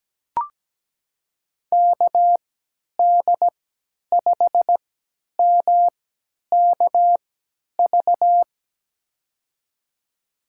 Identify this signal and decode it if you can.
Morse code, heard as a keyed tone, about 17 wpm, spaced as for 10 wpm, 700 Hz. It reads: KD5MKV
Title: USA Call Signs - 3x Character Spacing - 17wpm